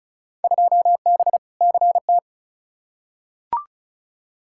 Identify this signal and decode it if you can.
Morse code, heard as a keyed tone, about 36 wpm, 700 Hz. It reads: 2B CT
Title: Field Day Exchanges Encore - 35wpm